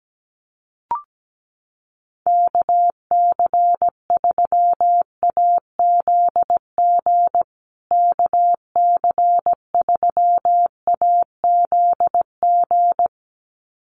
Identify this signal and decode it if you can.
Morse code, heard as a keyed tone, about 17 wpm, 700 Hz. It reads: KC3AZG KC3AZG
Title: USA Call Signs Repeated 2x 17wpm